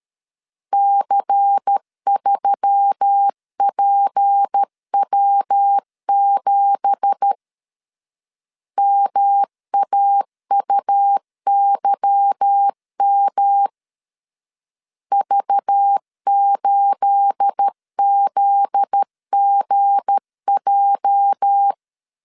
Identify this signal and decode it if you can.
Morse code, heard as a keyed tone, about 13 wpm, 800 Hz. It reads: C3PW7 MAUYM V8ZGJ
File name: CW / DET / 1 9 1500C S B 12wpm 3819